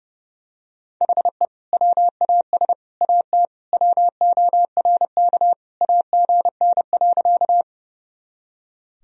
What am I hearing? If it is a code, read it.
Morse code, heard as a keyed tone, about 30 wpm, 700 Hz. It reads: HE WAS AT WORK AGN.
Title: Sentences from Top 200 Words and Abbreviations 30wpm